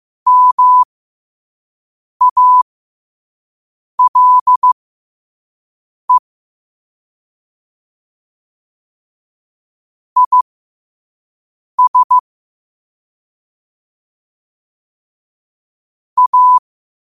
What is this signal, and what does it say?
Morse code, heard as a keyed tone, about 15 wpm, spaced as for 5 wpm, 1000 Hz. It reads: MALE IS A
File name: qotdfiles / 5wpm